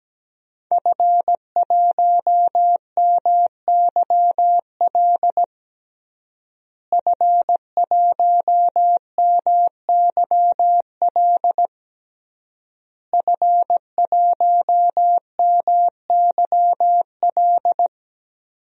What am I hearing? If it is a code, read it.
Morse code, heard as a keyed tone, about 17 wpm, 700 Hz. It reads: F1MYL F1MYL F1MYL
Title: DXCC-100 Call Signs - 5 Characters - Repeated 3x - 17wpm